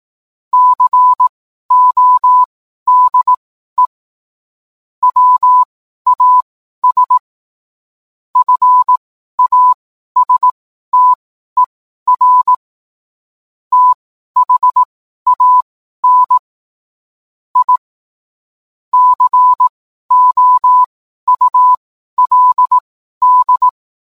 Morse code, heard as a keyed tone, about 18 wpm, spaced as for 12 wpm, 1000 Hz. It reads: CODE WAS FASTER THAN I COULD